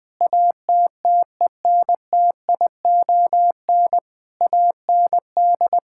Morse code, heard as a keyed tone, about 20 wpm, 700 Hz. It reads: ATTENTION AND
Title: Sentences from Top 1000 Words - Review Part 6 - 20wpm